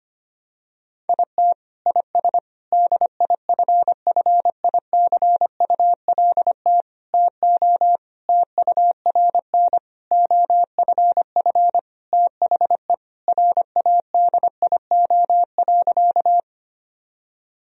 Morse code, heard as a keyed tone, about 25 wpm, 700 Hz. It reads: IT IS DIFFICULT TO TURN OFF THE RADIO.
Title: Sentences from Top 1000 Words - Review Part 3 - 25wpm